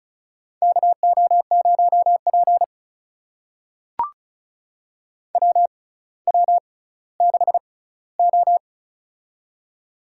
Morse code, heard as a keyed tone, about 36 wpm, spaced as for 18 wpm, 700 Hz. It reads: KO0P WW6O
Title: USA Call Signs - 4 Characters - 6x Character Spacing - 35wpm